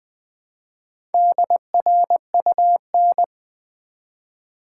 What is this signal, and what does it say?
Morse code, heard as a keyed tone, about 20 wpm, 700 Hz. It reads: DRUN
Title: Build A Word - 1 Letter At A Time - 5-Letter Words - 20wpm